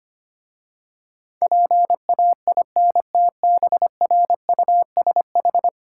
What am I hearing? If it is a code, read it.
Morse code, heard as a keyed tone, about 25 wpm, 690 Hz. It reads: PAINTBRUSH
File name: practice / 25wpm / Easy Words with Hints - 25wpm